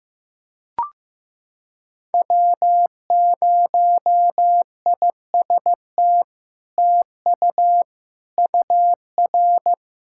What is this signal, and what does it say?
Morse code, heard as a keyed tone, about 15 wpm, 700 Hz. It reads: W0IST TU UR
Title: SOTA QSO Practice 15wpm